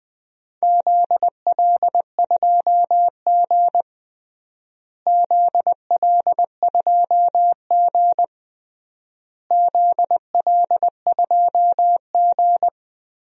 Morse code, heard as a keyed tone, about 20 wpm, 700 Hz. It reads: ZL2G ZL2G ZL2G
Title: DXCC-100 Call Signs - 4 Characters - Repeated 3x - 20wpm